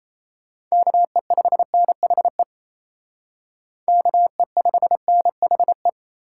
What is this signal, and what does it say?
Morse code, heard as a keyed tone, about 35 wpm, 700 Hz. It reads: KE5NHE KE5NHE
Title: Speed Racing - USA 2x3 Call Signs 28wpm